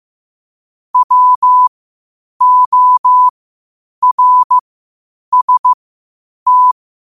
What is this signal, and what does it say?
Morse code, heard as a keyed tone, about 15 wpm, spaced as for 8 wpm, 1000 Hz. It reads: WORST
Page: Quotes for Fri, 15 Aug 2025 in Morse Code at 8 words per minute.